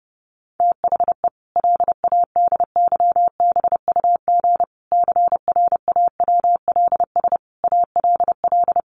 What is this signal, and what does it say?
Morse code, heard as a keyed tone, about 30 wpm, 700 Hz. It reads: THE LADYBUG CRAWLS ALL
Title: Additional Easy Phrases 30wpm